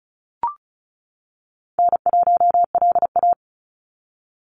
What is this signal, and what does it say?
Morse code, heard as a keyed tone, about 34 wpm, 700 Hz. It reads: N1LA
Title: USA Call Signs 35wpm